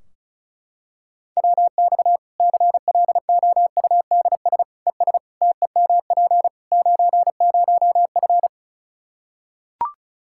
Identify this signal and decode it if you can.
Morse code, heard as a keyed tone, about 36 wpm, 700 Hz. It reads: WX CLOUDS ES TEMP 90F
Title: Practice QSO - Exchange 2 - Info - 35wpm